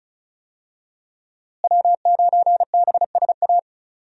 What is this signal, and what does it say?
Morse code, heard as a keyed tone, about 36 wpm, 700 Hz. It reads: W9BSA
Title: USA Call Signs - 5 Characters - Repeated 2x - 35wpm